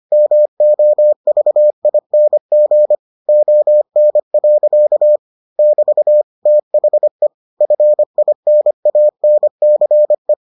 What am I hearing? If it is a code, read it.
Morse code, heard as a keyed tone, about 25 wpm, 600 Hz. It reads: MOVING ON. <BT> THE FINANCE